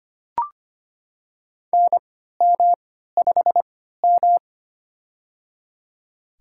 Morse code, heard as a keyed tone, about 25 wpm, spaced as for 14 wpm, 700 Hz. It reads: NM5M
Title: USA Call Signs - 4 Characters - 3x Character Spacing - 25wpm